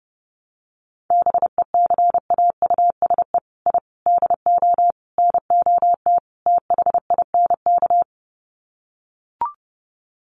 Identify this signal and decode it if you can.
Morse code, heard as a keyed tone, about 30 wpm, 700 Hz. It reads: BECAUSE I DO NOT THINK
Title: Sets of 5 Words Encore 30wpm